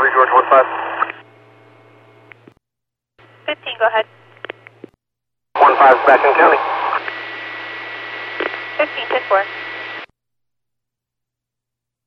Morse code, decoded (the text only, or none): none